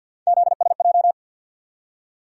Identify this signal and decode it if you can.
Morse code, heard as a keyed tone, about 53 wpm, 700 Hz. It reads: ZSJ
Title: USA Call Sign Postfixes - 3 Letters 50wpm